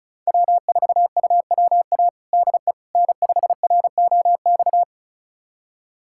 Morse code, heard as a keyed tone, about 36 wpm, 700 Hz. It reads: W4UWA DE N5ROX
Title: USA Callsign DE Callsign - 5 Characters - 35wpm